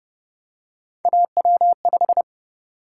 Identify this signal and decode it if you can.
Morse code, heard as a keyed tone, about 29 wpm, 700 Hz. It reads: AW5